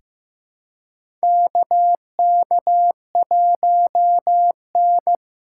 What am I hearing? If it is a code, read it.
Morse code, heard as a keyed tone, about 15 wpm, 700 Hz. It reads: KK1N
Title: USA Call Signs - 2x Character Spacing - 15wpm